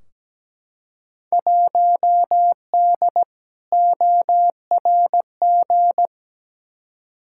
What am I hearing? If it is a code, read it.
Morse code, heard as a keyed tone, about 17 wpm, 700 Hz. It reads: 1D ORG
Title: Field Day Exchanges - 17wpm